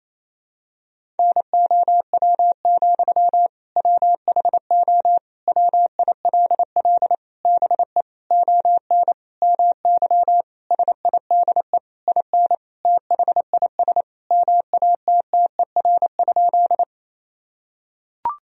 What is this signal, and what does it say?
Morse code, heard as a keyed tone, about 28 wpm, 700 Hz. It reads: NOW, WHO WILL BE ON MY SIDE IN THIS MATTER?
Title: Sentences from Top 1000 Words - Review Part 1 - 28wpm